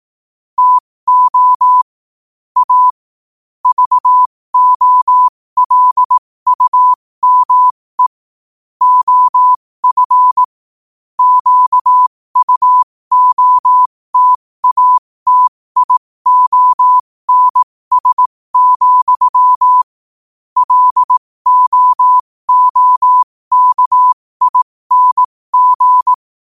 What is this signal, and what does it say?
Morse code, heard as a keyed tone, about 18 wpm, spaced as for 15 wpm, 1000 Hz. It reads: TO A VOLUME OF QUOTATIONS, LOOKING
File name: qotdfiles / 15wpm